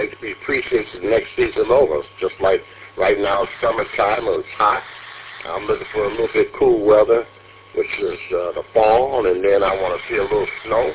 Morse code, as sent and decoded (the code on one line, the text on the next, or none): none